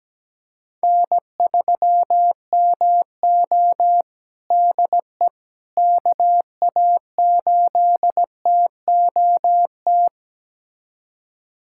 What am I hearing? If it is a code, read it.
Morse code, heard as a keyed tone, about 17 wpm, 700 Hz. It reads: N3MO DE KA8TOT
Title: USA Callsign DE Callsign 17wpm